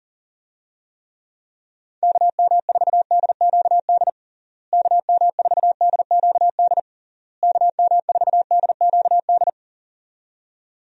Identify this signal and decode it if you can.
Morse code, heard as a keyed tone, about 37 wpm, 700 Hz. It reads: KM4DQD KM4DQD KM4DQD
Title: USA Call Signs - 6 Characters - Repeated 3x - 40wpm